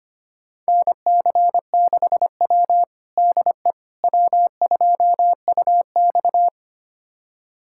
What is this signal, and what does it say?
Morse code, heard as a keyed tone, about 25 wpm, 700 Hz. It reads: NC6W DE W2UX
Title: USA Callsign DE Callsign - 4 Characters - 25wpm